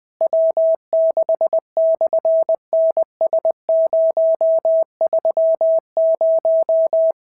Morse code, heard as a keyed tone, about 20 wpm, 690 Hz. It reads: W6/NS030
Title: SOTA QSO Practice 20wpm